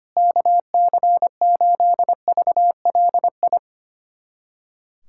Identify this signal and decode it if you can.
Morse code, heard as a keyed tone, about 25 wpm, 700 Hz. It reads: KC8VLI
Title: USA Callsign DE Callsign - 6 Characters - 25wpm